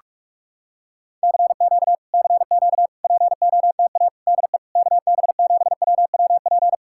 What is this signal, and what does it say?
Morse code, heard as a keyed tone, about 40 wpm, 700 Hz. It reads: CQ CQ POTA DE KB7WWP